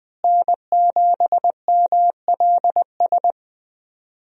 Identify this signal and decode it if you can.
Morse code, heard as a keyed tone, about 20 wpm, 700 Hz. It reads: N7MLS
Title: USA Call Signs - 5 Characters - Repeated 2x - 20wpm